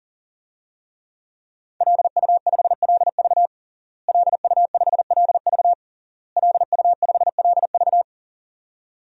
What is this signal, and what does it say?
Morse code, heard as a keyed tone, about 37 wpm, 700 Hz. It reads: LU5LV LU5LV LU5LV
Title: DXCC-100 Call Signs - 5 Characters - Repeated 3x - 40wpm